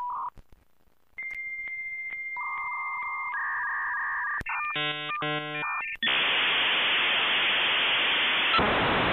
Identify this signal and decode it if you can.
Dial-up modem handshake.
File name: The Sound of dial-up InternetMp3Converter.net